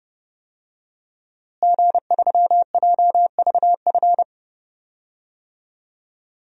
Morse code, heard as a keyed tone, about 30 wpm, 700 Hz. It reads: G3JVF
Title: DXCC-100 Call Signs - 5 Characters - Repeated 2x - 30wpm